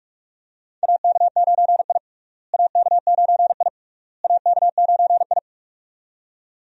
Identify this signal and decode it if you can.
Morse code, heard as a keyed tone, about 46 wpm, 700 Hz. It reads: AK9I AK9I AK9I
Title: USA Call Signs - 4 Characters - Repeated 3x - 45wpm